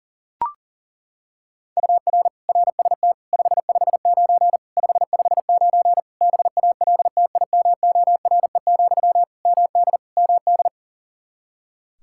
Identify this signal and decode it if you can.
Morse code, heard as a keyed tone, about 37 wpm, 700 Hz. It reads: UR RST 559 559 BALTIMORE, MD MD
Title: Practice QSO - Exchange 1 - Beginning - 40wpm